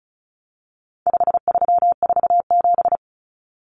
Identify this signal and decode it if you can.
Morse code, heard as a keyed tone, about 34 wpm, 700 Hz. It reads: 5347
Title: Common Cut Numbers - 4-Digit 35wpm